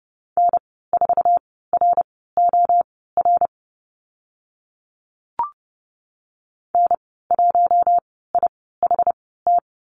Morse code, heard as a keyed tone, about 30 wpm, spaced as for 17 wpm, 700 Hz. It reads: N4ROR N1IHT
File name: Lesson 14g - Number 2 - Call Signs 30wpm